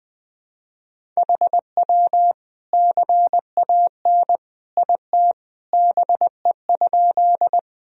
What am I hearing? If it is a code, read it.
Morse code, heard as a keyed tone, about 20 wpm, 700 Hz. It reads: HW CAN IT BE?
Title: Sentences from Top 100 Words and Abbreviations 20wpm